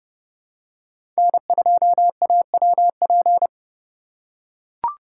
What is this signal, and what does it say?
Morse code, heard as a keyed tone, about 32 wpm, 700 Hz. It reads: N2AWP
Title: USA Call Signs - 5 Characters - 2x Character Spacing - 30wpm